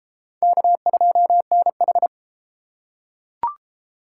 Morse code, heard as a keyed tone, about 31 wpm, 700 Hz. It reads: K2NH